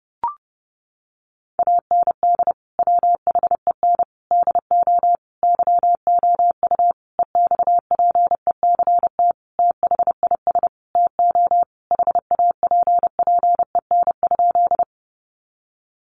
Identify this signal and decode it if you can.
Morse code, heard as a keyed tone, about 30 wpm, 700 Hz. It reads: AND WHEN DO YOU EXPECT THIS TO HAPPEN?
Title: Sentences from Top 1100 Words 30wpm